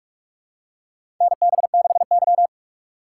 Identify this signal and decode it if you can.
Morse code, heard as a keyed tone, about 41 wpm, 700 Hz. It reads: NB6Y
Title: USA Call Signs - 4 Characters - 6x Character Spacing - 45wpm